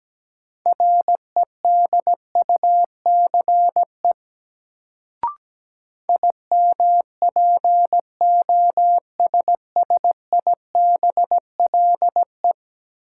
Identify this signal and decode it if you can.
Morse code, heard as a keyed tone, about 17 wpm, 700 Hz. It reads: REDUCE IMPOSSIBLE
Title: Sentences from Top 1100 Words - New Word Drill Alt - 17wpm